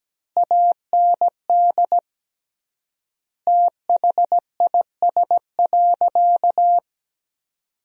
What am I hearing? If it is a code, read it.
Morse code, heard as a keyed tone, about 17 wpm, 700 Hz. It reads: AND THIS.
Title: 3x Word-Spacing - Sentences from Top 200 Words - 17wpm